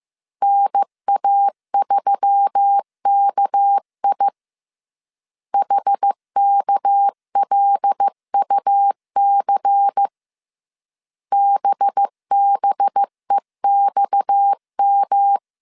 Morse code, heard as a keyed tone, about 15 wpm, 800 Hz. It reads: NA3KI HKLUC BBEXM
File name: CW / DET / 1 6 1500C S C 14wpm 2951